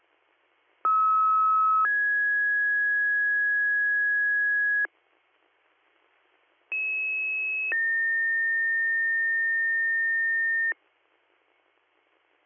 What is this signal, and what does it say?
Two-tone page, heard as a one-second tone, then a three-second tone, about 1290 then 1650 Hz, about 2600 then 1800 Hz. It is heard from 0.8 to 10.7 s.